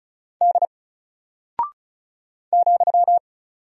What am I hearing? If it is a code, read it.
Morse code, heard as a keyed tone, about 36 wpm, 700 Hz. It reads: D ,
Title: Single Character (Letter:Number:Punctuation:Prosign) Alt 35wpm